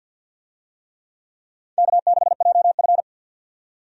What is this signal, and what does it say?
Morse code, heard as a keyed tone, about 50 wpm, 700 Hz. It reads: K6JF
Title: USA Call Signs - 4 Characters - 50wpm